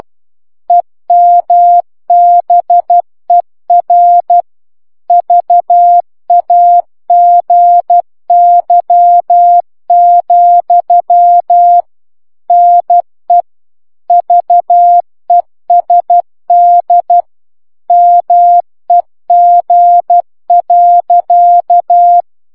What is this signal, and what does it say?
Morse code, heard as a keyed tone, about 12 wpm, 700 Hz. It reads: EMBER VAGY, NE VESD MEG.